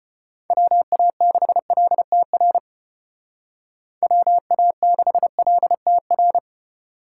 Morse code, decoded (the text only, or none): WA6LTR WA6LTR